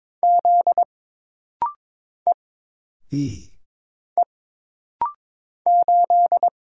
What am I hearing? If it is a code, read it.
Morse code, heard as a keyed tone, about 22 wpm, 700 Hz. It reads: Z E E 8
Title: Single Letter-Number Alt - Rapid-Fire 22wpm